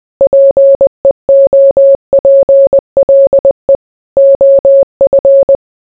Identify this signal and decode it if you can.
Morse code, heard as a keyed tone, about 20 wpm, 550 Hz. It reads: PEOPLE OF